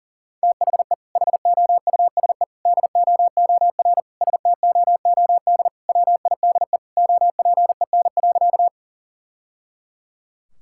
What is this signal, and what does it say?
Morse code, heard as a keyed tone, about 40 wpm, 700 Hz. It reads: THE HOUSE DOOR STOOD WIDE OPEN.
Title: Sentences from Top 1000 Words - Review Part 6 - 40wpm